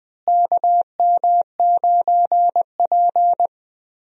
Morse code, heard as a keyed tone, about 20 wpm, 700 Hz. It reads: KM9P
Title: USA Call Signs - 4 Characters - Repeated 2x - 20wpm